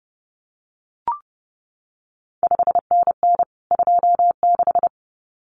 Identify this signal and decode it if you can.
Morse code, heard as a keyed tone, about 30 wpm, 700 Hz. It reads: 5NN 26
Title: CQ WW DX Contest Exchange Encore - 30wpm